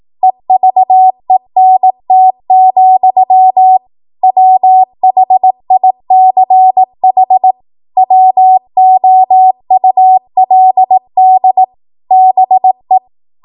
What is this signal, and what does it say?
Morse code, heard as a keyed tone, about 18 wpm, 750 Hz. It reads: EVENT, WHICH WOULD BE
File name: Archive / 18 WPM / 18WPM